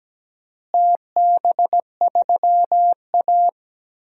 Morse code, heard as a keyed tone, about 17 wpm, 700 Hz. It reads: TB3A
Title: DXCC-100 Call Signs - 4 Characters - Repeated 2x - 17wpm